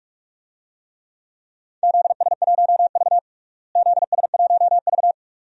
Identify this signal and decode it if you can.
Morse code, heard as a keyed tone, about 41 wpm, 700 Hz. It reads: ZS1V ZS1V
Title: DXCC-100 Call Signs - 4 Characters - Repeated 2x - 45wpm